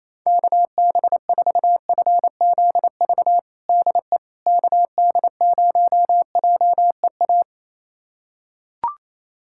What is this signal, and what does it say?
Morse code, heard as a keyed tone, about 28 wpm, 700 Hz. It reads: KB4FZV DE KD0JEA
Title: USA Callsign DE Callsign - 6 Characters - 28wpm